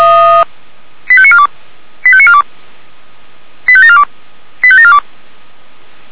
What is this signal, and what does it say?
Five-tone selective call (ZVEI), heard as quick runs of five tones, tones of about 70 ms. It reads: 85942 85942 85742 85742
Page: SDR/ZVEI